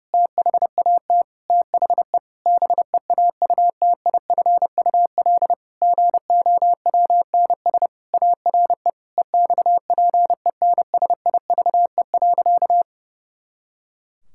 Morse code, heard as a keyed tone, about 30 wpm, 700 Hz. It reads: THAT THE BEAUTIFUL GOWNS ARE EXPENSIVE.
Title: Pangram Challenge v2 - 30wpm